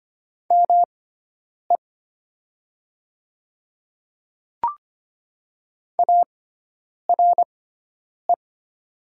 Morse code, heard as a keyed tone, about 25 wpm, spaced as for 9 wpm, 700 Hz. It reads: ME ARE
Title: 6x Character Spacing - Top 500 Words - 25wpm